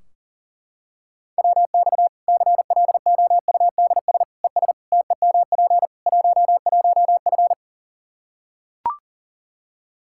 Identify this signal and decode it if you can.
Morse code, heard as a keyed tone, about 40 wpm, 700 Hz. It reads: WX CLOUDS ES TEMP 11F
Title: Practice QSO - Exchange 2 - Info - 40wpm